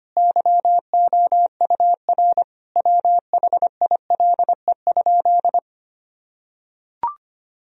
Morse code, heard as a keyed tone, about 25 wpm, 700 Hz. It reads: YOUR WHILE?
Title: Sentences from Top 1000 Words - Review Part 1 - 25wpm